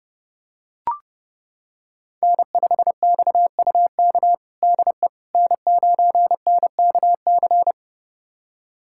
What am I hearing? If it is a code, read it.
Morse code, heard as a keyed tone, about 30 wpm, 700 Hz. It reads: N5XUK DE N9NKC
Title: USA Callsign DE Callsign - 5 Characters - 30wpm